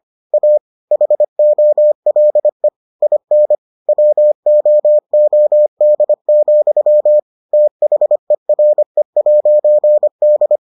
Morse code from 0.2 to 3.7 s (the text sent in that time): A HOLE IN